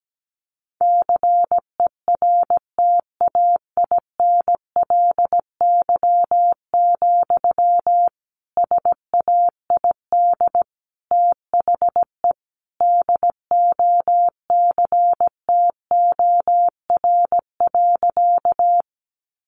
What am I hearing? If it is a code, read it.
Morse code, heard as a keyed tone, about 17 wpm, 700 Hz. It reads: CERTAINLY, SAID THE DOCTOR.
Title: Sentences from Top 1500 Words 17wpm